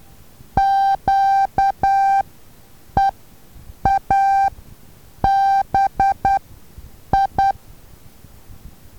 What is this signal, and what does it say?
Morse code, heard as a keyed tone, about 10 wpm, spaced as for 7 wpm, 780 Hz. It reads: QEABI
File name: Taviro HA6VA B